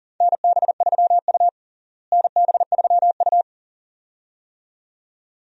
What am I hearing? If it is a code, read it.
Morse code, heard as a keyed tone, about 40 wpm, 700 Hz. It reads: NB3U NB3U
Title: USA Call Signs - 4 Characters - Repeated 2x - 40wpm